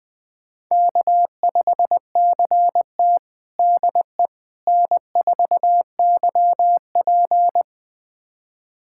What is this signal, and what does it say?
Morse code, heard as a keyed tone, about 20 wpm, 700 Hz. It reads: K5CT DE N4YP
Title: USA Callsign DE Callsign - 4 Characters - 20wpm